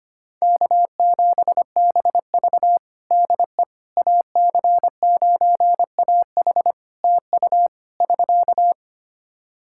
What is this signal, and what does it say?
Morse code, heard as a keyed tone, about 25 wpm, 700 Hz. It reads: K7BV DE AC9AH TU <SK>
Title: Practice QSO - Exchange 4 - Ending - 25wpm